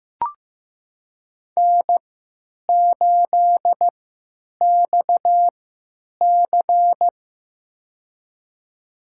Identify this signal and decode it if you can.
Morse code, heard as a keyed tone, about 15 wpm, spaced as for 9 wpm, 700 Hz. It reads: N8XC
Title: USA Call Signs - 4 Characters - 3x Character Spacing - 15wpm